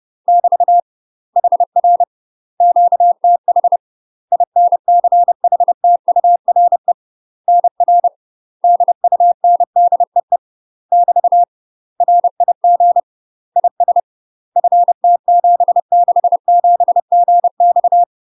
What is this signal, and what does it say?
Morse code, heard as a keyed tone, about 30 wpm, 700 Hz. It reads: <BT> HR QTH INCHTURE NR DUNDEE <BT> RIG IS FT767GX
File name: UK MASTER QSOs TRAINING / 30WPM / UK MASTER QSOs 0030